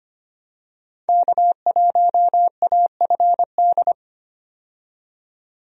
Morse code, heard as a keyed tone, about 25 wpm, 700 Hz. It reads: K1AFD
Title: USA Call Signs - 5 Characters - 25wpm